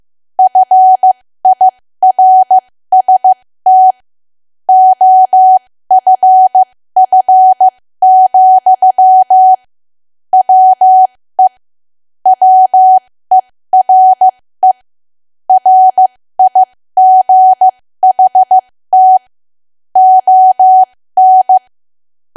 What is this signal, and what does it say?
Morse code, heard as a keyed tone, about 15 wpm, spaced as for 13 wpm, 750 Hz. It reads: FIRST OFF, WE WERE RIGHT ON
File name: cwa / W1AW-Level2 / 13WPM